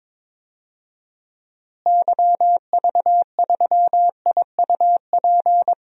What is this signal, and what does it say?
Morse code, heard as a keyed tone, about 22 wpm, 700 Hz. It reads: YV3IUP
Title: DXCC-100 Call Signs - 6 Characters - Repeated 3x - 22wpm